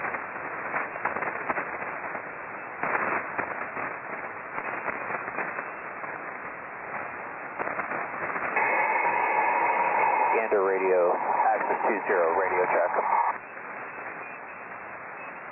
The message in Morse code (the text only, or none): none